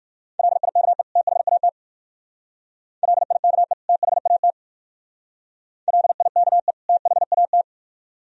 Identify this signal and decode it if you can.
Morse code, heard as a keyed tone, about 55 wpm, 700 Hz. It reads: LIKE THAT LIKE THAT LIKE THAT
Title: Speed-Racing - Sets of 2 Words 45wpm